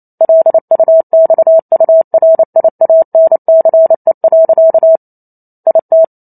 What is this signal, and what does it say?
Morse code, heard as a keyed tone, about 28 wpm, 650 Hz. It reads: LUXURIANCE. IT